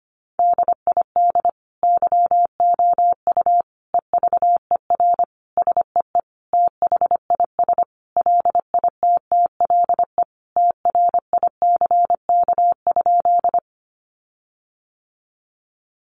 Morse code, heard as a keyed tone, about 25 wpm, 700 Hz. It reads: DID YOU EVER SEE THIS LITTLE TRICK?
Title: Sentences from Top 1500 Words 25wpm